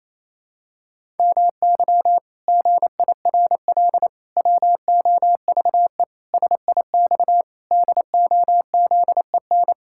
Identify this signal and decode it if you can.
Morse code, heard as a keyed tone, about 28 wpm, 700 Hz. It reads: MY GIRL WOVE SIX DOZEN
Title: Pangram Challenge v2 - 28wpm